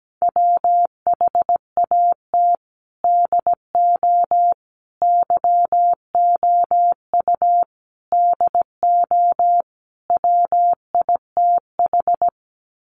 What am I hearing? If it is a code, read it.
Morse code, heard as a keyed tone, about 17 wpm, 700 Hz. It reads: WHAT DO YOU DO WITH
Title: Sets of 5 Words Encore2 17wpm